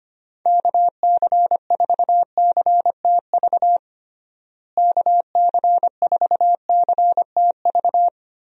Morse code, heard as a keyed tone, about 25 wpm, 700 Hz. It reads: KC4CTV KC4CTV
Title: USA Call Signs - 6 Characters - Repeated 2x - 25wpm